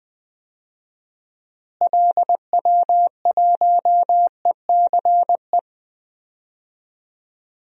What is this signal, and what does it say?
Morse code, heard as a keyed tone, about 20 wpm, 700 Hz. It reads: LW1ECE
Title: DXCC-100 Call Signs - 6 Characters - Repeated 3x - 20wpm